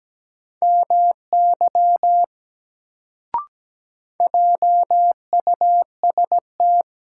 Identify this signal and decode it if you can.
Morse code, heard as a keyed tone, about 17 wpm, 700 Hz. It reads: MY JUST
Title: Sentences from Top 200 Words - New Word Drill - 17wpm